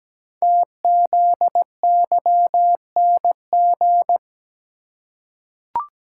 Morse code, heard as a keyed tone, about 17 wpm, 700 Hz. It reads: TZYNG
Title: Code Groups - 5 Letter:Numbers - 17wpm